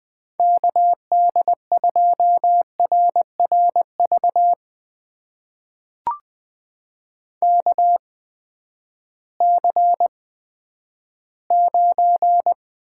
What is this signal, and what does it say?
Morse code, heard as a keyed tone, about 19 wpm, spaced as for 11 wpm, 700 Hz. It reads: KD2RRV KC9
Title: USA Call Signs - 6 Characters - 8x Character Spacing - 20wpm